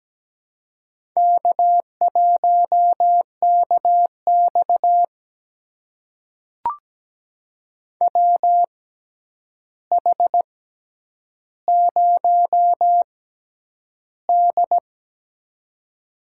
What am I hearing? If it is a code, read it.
Morse code, heard as a keyed tone, about 17 wpm, spaced as for 9 wpm, 700 Hz. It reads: K1KX WH0D
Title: USA Call Signs - 4 Characters - 6x Character Spacing - 17wpm